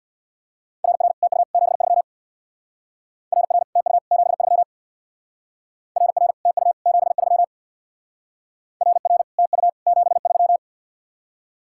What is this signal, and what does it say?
Morse code, heard as a keyed tone, about 62 wpm, 690 Hz. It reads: RR TU 73 RR TU 73 RR TU 73 RR TU 73
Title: Speed-Racing Common QSO Phrases - 50wpm